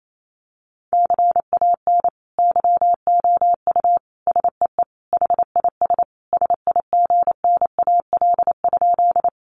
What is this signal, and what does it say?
Morse code, heard as a keyed tone, about 28 wpm, 700 Hz. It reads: CAN YOU SEE HIS SIGNAL?
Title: Sentences from Top 1400 Words 28wpm